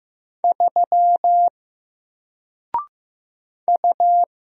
Morse code, heard as a keyed tone, about 15 wpm, 700 Hz. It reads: 3 U
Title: Single Letter-Number 15wpm